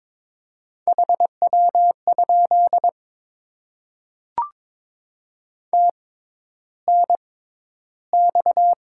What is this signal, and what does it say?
Morse code, heard as a keyed tone, about 22 wpm, spaced as for 11 wpm, 700 Hz. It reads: HW? TNX
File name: Top 20 QSO Elements - 6x Character Spacing - 22wpm